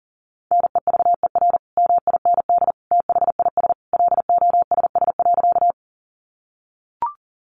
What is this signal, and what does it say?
Morse code, heard as a keyed tone, about 40 wpm, 700 Hz. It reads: NEVER MIND THIS LOSS.
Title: Sentences from Top 1400 Words 40wpm